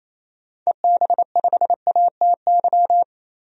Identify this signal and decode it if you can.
Morse code, heard as a keyed tone, about 28 wpm, 700 Hz. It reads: EB5ATY
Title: DXCC-100 Call Signs - 6 Characters - 28wpm